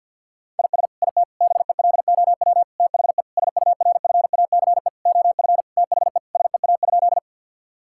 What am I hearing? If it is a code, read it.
Morse code, heard as a keyed tone, about 50 wpm, 700 Hz. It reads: IS IT BELOW THE SURFACE OF THE SEA?
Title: Sentences from Top 1000 Words - Review Part 5 - 50wpm